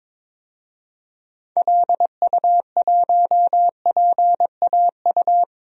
Morse code, heard as a keyed tone, about 21 wpm, 700 Hz. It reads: LU1PAU
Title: DXCC-100 Call Signs - 6 Characters - 22wpm